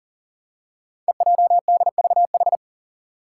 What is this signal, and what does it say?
Morse code, heard as a keyed tone, about 41 wpm, 700 Hz. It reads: EJDVH
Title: Five Challenging Letter-Number Combinations 40wpm